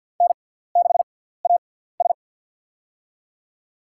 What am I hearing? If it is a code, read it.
Morse code, heard as a keyed tone, about 50 wpm, spaced as for 17 wpm, 700 Hz. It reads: N6AS
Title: USA Call Signs - 4 Characters - 6x Character Spacing - 50wpm